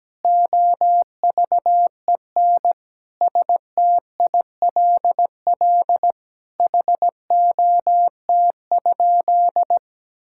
Morse code, heard as a keyed tone, about 17 wpm, 700 Hz. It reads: OVEN STILL HOT?
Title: Sentences from Top 1000 Words - Review Part 6 - 17wpm